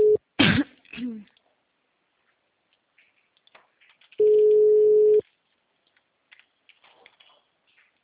Ringback tone (Europe, 425 Hz).
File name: out- -2012- - - .24970